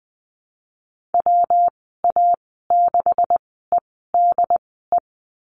Morse code, heard as a keyed tone, about 20 wpm, spaced as for 15 wpm, 700 Hz. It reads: WA6EDE
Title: 147 Lesson 24g - Letter y - Call Signs 20wpm